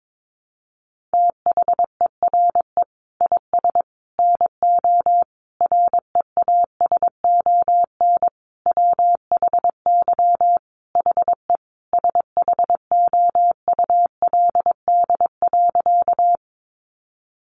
Morse code, heard as a keyed tone, about 22 wpm, 700 Hz. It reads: THERE IS NO REASON WHY HE SHOULD.
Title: Sentences from Top 700 Words 22wpm